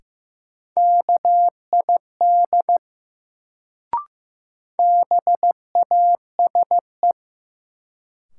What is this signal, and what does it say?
Morse code, heard as a keyed tone, about 15 wpm, 700 Hz. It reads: KID BASE
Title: Sentences from Top 1600 Words - New Word Drill Alt - 15wpm